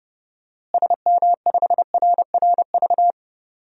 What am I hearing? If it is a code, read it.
Morse code, heard as a keyed tone, about 30 wpm, 700 Hz. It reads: SM5RRV
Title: DXCC-100 Call Signs - 6 Characters - Repeated 3x - 30wpm